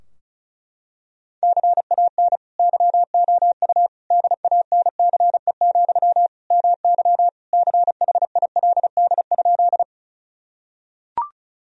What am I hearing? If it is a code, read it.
Morse code, heard as a keyed tone, about 34 wpm, 700 Hz. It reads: CAN YOU DANCE, MY CHILD?
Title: Sentences from Top 1000 Words - Review Part 4 - 35wpm